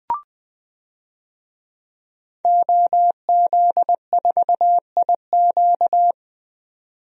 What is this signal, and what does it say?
Morse code, heard as a keyed tone, about 20 wpm, 700 Hz. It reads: OZ4IQ
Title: DXCC-100 Call Signs - 5 Characters - 20wpm